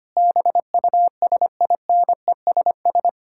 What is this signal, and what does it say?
Morse code, heard as a keyed tone, about 25 wpm, 700 Hz. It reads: BUSINESS
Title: Sentences from Top 1000 Words - Review Part 6 - 25wpm